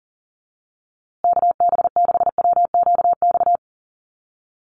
Morse code, heard as a keyed tone, about 40 wpm, 700 Hz. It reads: KB6WQX
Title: USA Call Signs Repeated 3x 40wpm